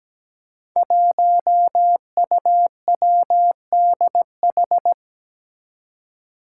Morse code, heard as a keyed tone, about 17 wpm, 700 Hz. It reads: 1UWDH
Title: Five Challenging Letter-Number Combinations 17wpm